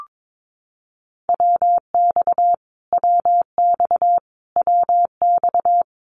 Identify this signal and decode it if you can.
Morse code, heard as a keyed tone, about 22 wpm, 700 Hz. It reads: WX WX WX
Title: Abbreviations Repeated 3x - 22wpm